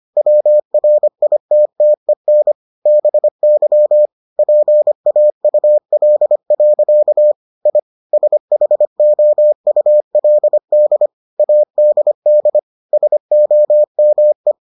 Morse code, heard as a keyed tone, about 25 wpm, 600 Hz. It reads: WRITTEN BY PAUL. I SHOULD ADD SOME